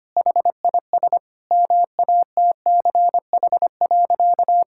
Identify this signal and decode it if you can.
Morse code, heard as a keyed tone, about 25 wpm, 700 Hz. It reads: HIS MATCH.
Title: Sentences from Top 1600 Words 25wpm